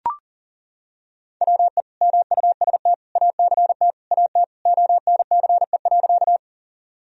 Morse code, heard as a keyed tone, about 40 wpm, 700 Hz. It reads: WE MUST ACT AT ONCE.
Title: Sentences from Top 1000 Words - Review Part 1 - 40wpm